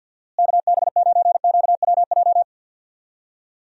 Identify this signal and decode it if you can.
Morse code, heard as a keyed tone, about 46 wpm, 700 Hz. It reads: KB9QWJ
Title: USA Call Signs - 6 Characters - Repeated 2x - 50wpm